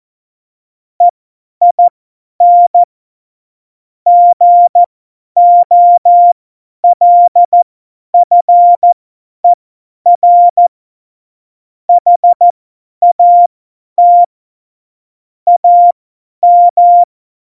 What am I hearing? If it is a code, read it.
Morse code, heard as a keyed tone, about 14 wpm, spaced as for 10 wpm, 700 Hz. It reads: EIN GOLFER HAT AM
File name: Morsekurs / Witze / Witz 26. Januar 2021, 50 F70